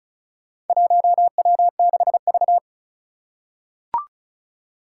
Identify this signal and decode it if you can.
Morse code, heard as a keyed tone, about 36 wpm, 700 Hz. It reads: 1W6V
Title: Four Challenging Letter-Number Combinations 35wpm